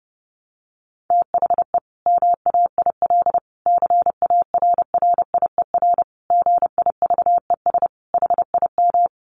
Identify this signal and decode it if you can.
Morse code, heard as a keyed tone, about 30 wpm, 700 Hz. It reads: THE MAIL CARRIER GIVES HIM
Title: Additional Easy Phrases 30wpm